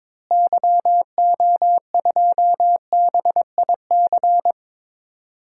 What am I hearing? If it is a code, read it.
Morse code, heard as a keyed tone, about 21 wpm, 700 Hz. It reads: YO2BIC
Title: DXCC-100 Call Signs - 6 Characters - Repeated 3x - 22wpm